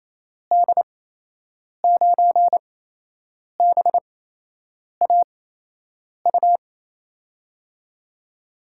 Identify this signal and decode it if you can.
Morse code, heard as a keyed tone, about 28 wpm, spaced as for 8 wpm, 700 Hz. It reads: D9BAU
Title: USA Call Signs - 8x Character Spacing - 28wpm